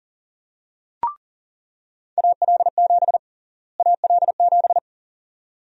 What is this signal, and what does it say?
Morse code, heard as a keyed tone, about 40 wpm, 700 Hz. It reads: AL7 AL7
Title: USA Call Sign Prefixes - Repeated 2x - 40wpm